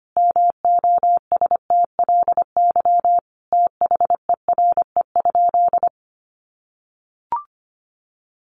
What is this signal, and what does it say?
Morse code, heard as a keyed tone, about 25 wpm, 700 Hz. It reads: MOSTLY THERE?
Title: Sentences from Top 1300 Words 25wpm